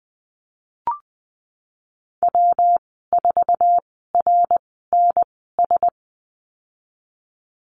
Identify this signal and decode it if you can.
Morse code, heard as a keyed tone, about 20 wpm, spaced as for 15 wpm, 700 Hz. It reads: W4RNS
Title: 161 Lesson 26g - Letter g - Call Signs 20wpm